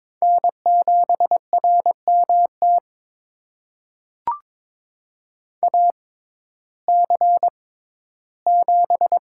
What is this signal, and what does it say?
Morse code, heard as a keyed tone, about 22 wpm, spaced as for 13 wpm, 700 Hz. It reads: N7RMT AC7
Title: USA Call Signs - 5 Characters - 6x Character Spacing - 22wpm